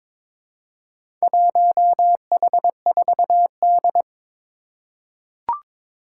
Morse code, heard as a keyed tone, about 22 wpm, 700 Hz. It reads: 1H4D
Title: Four Challenging Letter-Number Combinations 22wpm